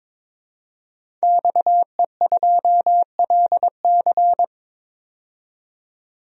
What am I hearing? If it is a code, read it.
Morse code, heard as a keyed tone, about 22 wpm, 700 Hz. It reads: XE2LC
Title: DXCC-100 Call Signs - 5 Characters - 22wpm